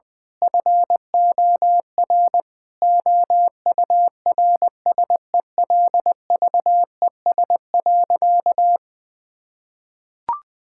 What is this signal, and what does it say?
Morse code, heard as a keyed tone, about 21 wpm, 700 Hz. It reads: FOR OURSELVES.
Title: Sentences from Top 1600 Words 20wpm